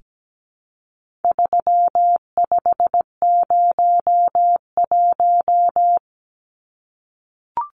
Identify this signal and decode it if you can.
Morse code, heard as a keyed tone, about 17 wpm, 700 Hz. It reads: 3501
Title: Numbers - 4-Digit 17wpm